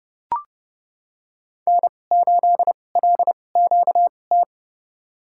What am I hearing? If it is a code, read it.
Morse code, heard as a keyed tone, about 30 wpm, spaced as for 22 wpm, 700 Hz. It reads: N8LQT
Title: USA Call Signs - 5 Characters - 2x Character Spacing - 30wpm